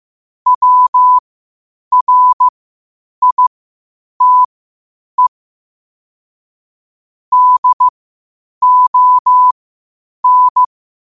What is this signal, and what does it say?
Morse code, heard as a keyed tone, about 15 wpm, spaced as for 8 wpm, 1000 Hz. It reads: WRITE DON
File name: qotdfiles / 8wpm